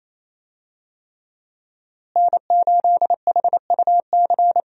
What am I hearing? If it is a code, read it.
Morse code, heard as a keyed tone, about 28 wpm, 700 Hz. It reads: N8HUC
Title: USA Call Signs - 5 Characters - Repeated 3x - 28wpm